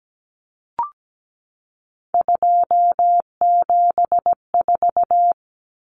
Morse code, heard as a keyed tone, about 17 wpm, 700 Hz. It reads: 274
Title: Numbers - 3-Digit 17wpm